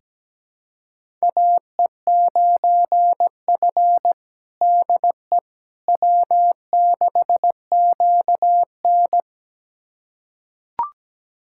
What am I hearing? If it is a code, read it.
Morse code, heard as a keyed tone, about 17 wpm, 700 Hz. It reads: AE9F DE W6QN
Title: USA Callsign DE Callsign - 4 Characters - 17wpm